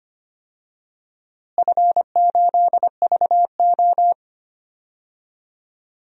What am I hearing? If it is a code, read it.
Morse code, heard as a keyed tone, about 25 wpm, 700 Hz. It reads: F8VO
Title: DXCC-100 Call Signs - 4 Characters - Repeated 3x - 25wpm